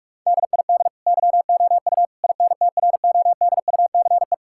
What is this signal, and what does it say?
Morse code, heard as a keyed tone, about 46 wpm, 700 Hz. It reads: DID YOU INTRODUCE
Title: Sentences from Top 1600 Words 45wpm